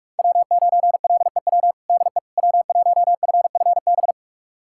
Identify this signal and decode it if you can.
Morse code, heard as a keyed tone, about 46 wpm, 700 Hz. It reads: W9LEW DE W1FFB
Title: USA Callsign DE Callsign - 5 Characters - 45wpm